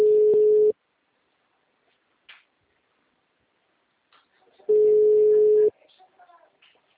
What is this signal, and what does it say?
Ringback tone (Europe, 425 Hz), with speech.
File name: out- -2034- - - .74921